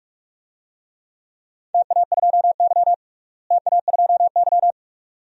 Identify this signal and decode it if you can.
Morse code, heard as a keyed tone, about 47 wpm, 700 Hz. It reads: TA2Y TA2Y
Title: DXCC-100 Call Signs - 4 Characters - Repeated 2x - 45wpm